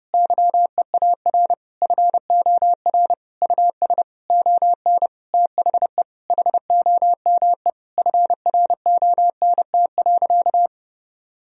Morse code, heard as a keyed tone, about 30 wpm, 700 Hz. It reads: YEAR FOR US ON THE HOME FRONT.
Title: Sentences from Top 1000 Words - Review Part 2 - 30wpm